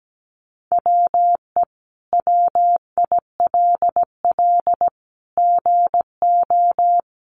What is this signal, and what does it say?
Morse code, heard as a keyed tone, about 17 wpm, 700 Hz. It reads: WE WILL GO
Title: Sentences from Top 1500 Words 17wpm